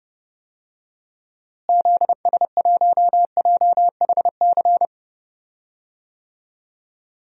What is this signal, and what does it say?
Morse code, heard as a keyed tone, about 30 wpm, 700 Hz. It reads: ZS1JHC
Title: DXCC-100 Call Signs - 6 Characters - Repeated 3x - 30wpm